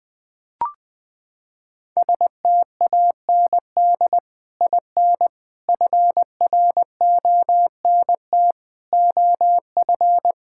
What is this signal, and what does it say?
Morse code, heard as a keyed tone, about 20 wpm, 700 Hz. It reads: STAND IN FRONT OF
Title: Sentences with 2-Word Phrases - 20wpm